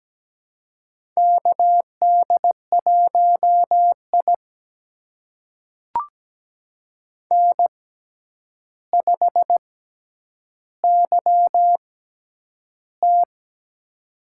Morse code, heard as a keyed tone, about 17 wpm, spaced as for 9 wpm, 700 Hz. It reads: KD1I N5YT
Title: USA Call Signs - 4 Characters - 6x Character Spacing - 17wpm